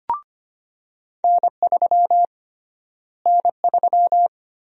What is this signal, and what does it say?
Morse code, heard as a keyed tone, about 25 wpm, 700 Hz. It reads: N3 N3
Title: USA Call Sign Prefixes - Repeated 2x - 25wpm